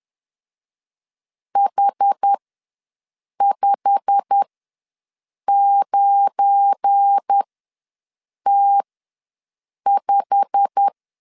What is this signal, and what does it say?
Morse code, heard as a keyed tone, about 10 wpm, spaced as for 6 wpm, 800 Hz. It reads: H59T5
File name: LEZIONI / 8 YJ45 / 8 YJ45 ws 10wpm 3843